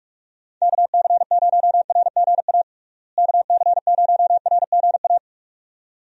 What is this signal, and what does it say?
Morse code, heard as a keyed tone, about 46 wpm, 700 Hz. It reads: KC0RGA KC0RGA
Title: USA Call Signs - 6 Characters - Repeated 2x - 45wpm